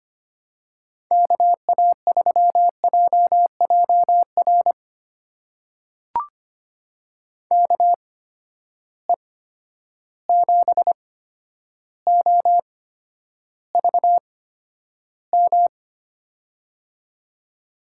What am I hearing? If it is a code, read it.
Morse code, heard as a keyed tone, about 25 wpm, spaced as for 11 wpm, 700 Hz. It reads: KA3JJR KE7OVM
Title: USA Call Signs - 6 Characters - 8x Character Spacing - 25wpm